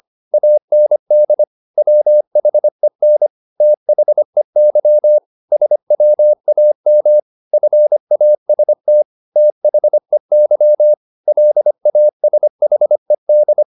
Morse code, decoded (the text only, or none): AND WHEN THEY SWAM FAST THEY LASHED